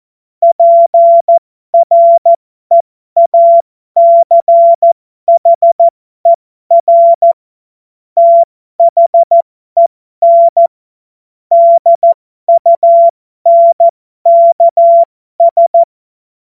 Morse code, heard as a keyed tone, about 14 wpm, spaced as for 12 wpm, 680 Hz. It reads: PREACHER THEN DUNKS